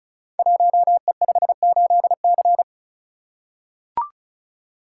Morse code, heard as a keyed tone, about 36 wpm, 700 Hz. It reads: 1E58C
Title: Code Groups - 5 Letter:Numbers - 35wpm